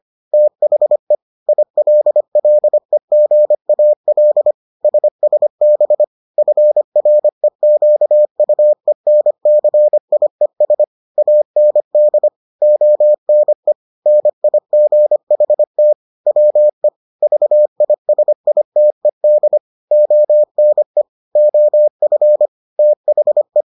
Morse code, heard as a keyed tone, about 25 wpm, 600 Hz. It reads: THE ILLEGAL SSB FREQUENCIES AND ONE NIGHT WE VISITED ONE OF THE